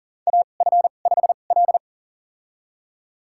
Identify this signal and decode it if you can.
Morse code, heard as a keyed tone, about 39 wpm, spaced as for 28 wpm, 690 Hz. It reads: AF5L
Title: USA Call Signs - 4 Characters - 2x Character Spacing - 40wpm